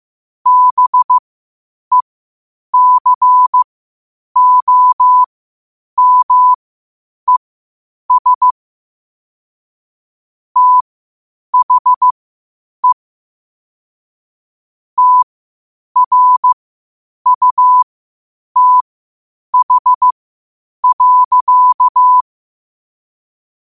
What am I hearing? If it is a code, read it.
Morse code, heard as a keyed tone, about 15 wpm, spaced as for 8 wpm, 1000 Hz. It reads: BECOMES THE TRUTH.
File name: qotdfiles / 8wpm